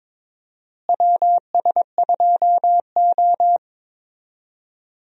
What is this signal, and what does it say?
Morse code, heard as a keyed tone, about 22 wpm, 700 Hz. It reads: WS2O
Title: USA Call Signs - 4 Characters - Repeated 3x - 22wpm